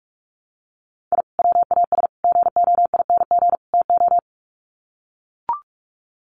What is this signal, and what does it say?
Morse code, heard as a keyed tone, about 44 wpm, 700 Hz. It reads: I WAS GOING TO
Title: Sets of 4 Words 45wpm